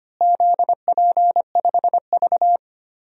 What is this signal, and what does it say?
Morse code, heard as a keyed tone, about 25 wpm, 700 Hz. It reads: ZP5V
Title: DXCC-100 Call Signs - 4 Characters - Repeated 3x - 25wpm